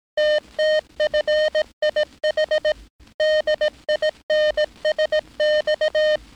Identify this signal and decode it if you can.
Morse code, heard as a keyed tone, about 18 wpm, 610 Hz. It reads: TTFIH DINSX
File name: training / NAVPERS 11049to11060 / NAVPERS 11059 RA RB